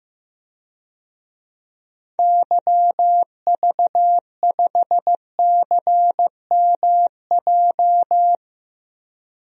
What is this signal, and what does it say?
Morse code, heard as a keyed tone, about 15 wpm, 700 Hz. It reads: YV5CMJ
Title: DXCC-100 Call Signs - 6 Characters - 15wpm